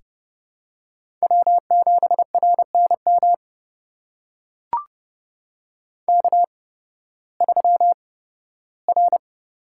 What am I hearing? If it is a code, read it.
Morse code, heard as a keyed tone, about 30 wpm, spaced as for 16 wpm, 700 Hz. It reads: W7RNM K3R
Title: USA Call Signs - 5 Characters - 8x Character Spacing - 30wpm